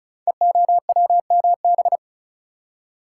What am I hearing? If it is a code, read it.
Morse code, heard as a keyed tone, about 34 wpm, 700 Hz. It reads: EOWMB
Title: Code Groups - 5 Letter:Numbers - 35wpm